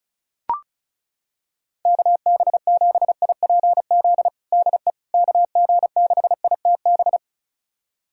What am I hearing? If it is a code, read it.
Morse code, heard as a keyed tone, about 36 wpm, 700 Hz. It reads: KB7IPZ DE KG6ITB
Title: USA Callsign DE Callsign 35wpm